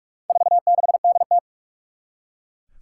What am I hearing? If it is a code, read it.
Morse code, heard as a keyed tone, about 41 wpm, 700 Hz. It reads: 46DT
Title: Four Challenging Letter-Number Combinations 45wpm